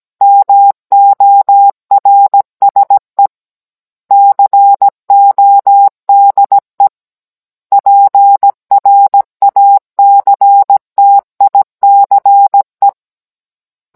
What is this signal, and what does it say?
Morse code, heard as a keyed tone, about 17 wpm, 800 Hz. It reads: MORSE CODE PRACTICE